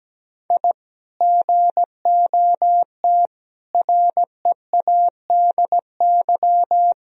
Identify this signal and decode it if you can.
Morse code, heard as a keyed tone, about 17 wpm, 700 Hz. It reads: I GOT READY
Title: Sentences from Top 1000 Words - Review Part 3 - 17wpm